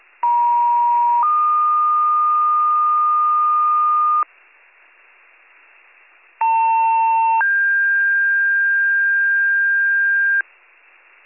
Two-tone page, heard as a one-second tone, then a three-second tone, about 950 then 1220 Hz, about 900 then 1600 Hz. It is heard from 0.2 to 10.4 s.